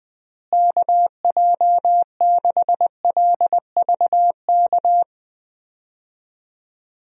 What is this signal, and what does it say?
Morse code, heard as a keyed tone, about 21 wpm, 700 Hz. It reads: KJ6LVK
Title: USA Call Signs - 6 Characters - Repeated 3x - 20wpm